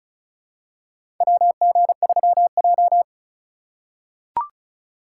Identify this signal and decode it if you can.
Morse code, heard as a keyed tone, about 36 wpm, 700 Hz. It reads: WG3J
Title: USA Call Signs - 6x Character Spacing - 35wpm